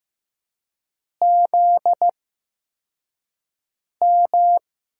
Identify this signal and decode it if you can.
Morse code, heard as a keyed tone, about 15 wpm, spaced as for 4 wpm, 700 Hz. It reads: ZM
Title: USA Call Signs - 4 Characters - 8x Character Spacing - 15wpm